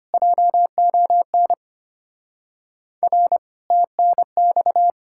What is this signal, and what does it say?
Morse code, heard as a keyed tone, about 33 wpm, 700 Hz. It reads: JON R TNX